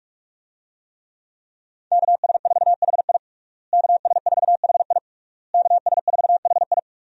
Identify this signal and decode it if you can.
Morse code, heard as a keyed tone, about 46 wpm, 700 Hz. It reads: KS4HI KS4HI KS4HI
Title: USA Call Signs - 5 Characters - Repeated 3x - 45wpm